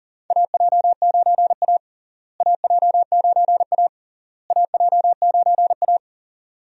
Morse code, heard as a keyed tone, about 40 wpm, 700 Hz. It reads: AJ9A AJ9A AJ9A
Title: USA Call Signs - 4 Characters - Repeated 3x - 40wpm